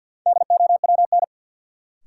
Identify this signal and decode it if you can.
Morse code, heard as a keyed tone, about 53 wpm, 700 Hz. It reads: DOWN
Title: Sentences from Top 300 Words - New Word Drill - 50wpm